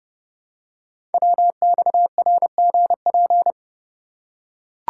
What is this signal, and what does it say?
Morse code, heard as a keyed tone, about 30 wpm, 700 Hz. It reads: WXRGP
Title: Code Groups - 5 Letters - 30wpm